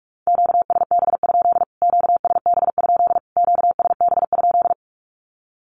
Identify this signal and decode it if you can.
Morse code, heard as a keyed tone, about 44 wpm, 700 Hz. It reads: QSB? QSB? QSB?
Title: Q-Codes Repeated 3x 45wpm